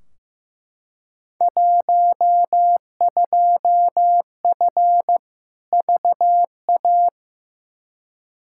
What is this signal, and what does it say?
Morse code, heard as a keyed tone, about 15 wpm, 700 Hz. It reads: 12F VA
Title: Field Day Exchanges Encore - 15wpm